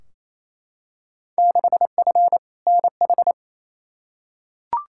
Morse code, heard as a keyed tone, about 28 wpm, 700 Hz. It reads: 6F NH
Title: Field Day Exchanges Encore - 28wpm